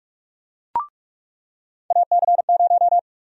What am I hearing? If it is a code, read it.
Morse code, heard as a keyed tone, about 46 wpm, 700 Hz. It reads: AC0
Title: USA Call Sign Prefixes - 45wpm